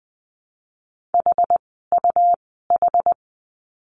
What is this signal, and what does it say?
Morse code, heard as a keyed tone, about 20 wpm, spaced as for 15 wpm, 700 Hz. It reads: HUH
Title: 101 Lesson 17f - Letter u - Words 20wpm